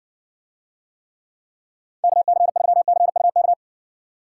Morse code, heard as a keyed tone, about 60 wpm, 700 Hz. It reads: KC3CUK